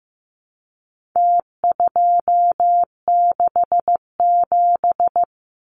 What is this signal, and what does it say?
Morse code, heard as a keyed tone, about 15 wpm, 700 Hz. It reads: T267
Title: Numbers - Contest Style - 4-Digit 15wpm